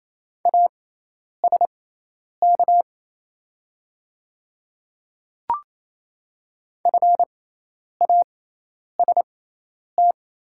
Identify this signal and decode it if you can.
Morse code, heard as a keyed tone, about 28 wpm, spaced as for 10 wpm, 690 Hz. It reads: ASK FAST
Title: 6x Character Spacing - Top 500 Words - 28wpm